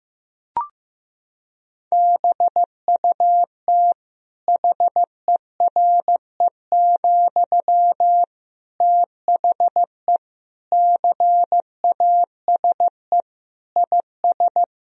Morse code, heard as a keyed tone, about 15 wpm, 700 Hz. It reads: BUT HERE, THE CASE IS
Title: Sentences from Top 1000 Words - Review Part 3 - 15wpm